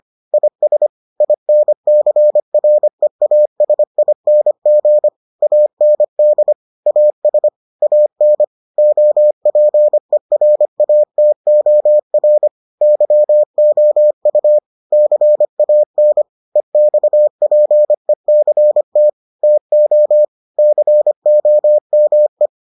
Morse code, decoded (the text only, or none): IS INCREASING AND AS AN OPERATOR YOU CAN EXPECT TO COME